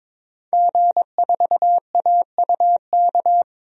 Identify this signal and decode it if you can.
Morse code, heard as a keyed tone, about 21 wpm, 700 Hz. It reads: G4AUK
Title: DXCC-100 Call Signs - 5 Characters - Repeated 3x - 22wpm